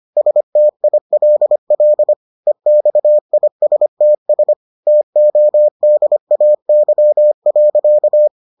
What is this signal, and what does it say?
Morse code, heard as a keyed tone, about 25 wpm, 600 Hz. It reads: STILL EXISTS TODAY.